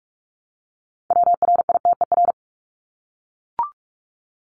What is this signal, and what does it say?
Morse code, heard as a keyed tone, about 44 wpm, 700 Hz. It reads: WRITER
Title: Top 1751-2000 Words 45wpm